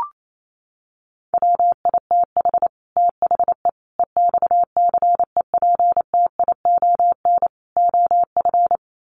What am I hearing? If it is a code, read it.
Morse code, heard as a keyed tone, about 28 wpm, 700 Hz. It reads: WITH THE EXCEPTION OF
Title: Sets of 4 Words 28wpm